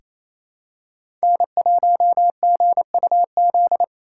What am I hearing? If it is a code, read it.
Morse code, heard as a keyed tone, about 28 wpm, 700 Hz. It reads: N1GUZ
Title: USA Callsign DE Callsign 28wpm